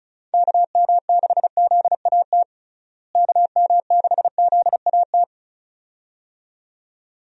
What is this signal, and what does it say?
Morse code, heard as a keyed tone, about 36 wpm, 700 Hz. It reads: KM6ZAT KM6ZAT
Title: USA Call Signs - 6 Characters - Repeated 2x - 35wpm